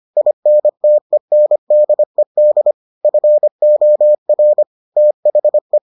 Morse code, heard as a keyed tone, about 25 wpm, 600 Hz. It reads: INTENDED FOR THE